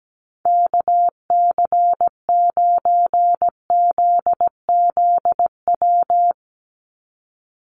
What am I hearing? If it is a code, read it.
Morse code, heard as a keyed tone, about 17 wpm, 700 Hz. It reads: KC9ZZW
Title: USA Call Signs 17wpm